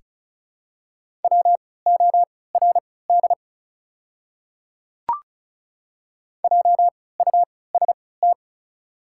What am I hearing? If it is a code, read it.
Morse code, heard as a keyed tone, about 36 wpm, spaced as for 20 wpm, 700 Hz. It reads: WORD JUST
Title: 3x Character Spacing - Top 200 Words - 35wpm